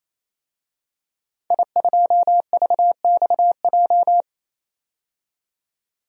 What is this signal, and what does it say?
Morse code, heard as a keyed tone, about 28 wpm, 700 Hz. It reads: I2VXJ
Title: DXCC-100 Call Signs - 5 Characters - 28wpm